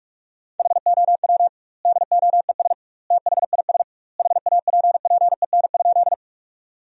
Morse code, heard as a keyed tone, about 41 wpm, 700 Hz. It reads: HOW DOES THIS HAPPEN?
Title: 2x Word-Spacing - Sentences from Top 400 Words - 45wpm